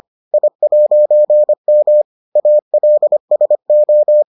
Morse code, heard as a keyed tone, about 25 wpm, 600 Hz. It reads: I'M ALSO